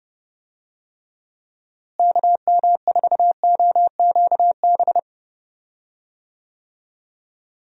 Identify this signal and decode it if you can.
Morse code, heard as a keyed tone, about 30 wpm, 700 Hz. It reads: KM4OQB
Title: USA Call Signs - 6 Characters - Repeated 2x - 30wpm